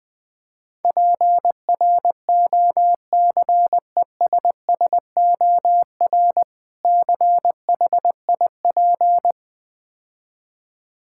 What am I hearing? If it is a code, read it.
Morse code, heard as a keyed tone, about 21 wpm, 700 Hz. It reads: PROCESSOR CHIP
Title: Difficult Words - Speed-Racing with Hints - 20wpm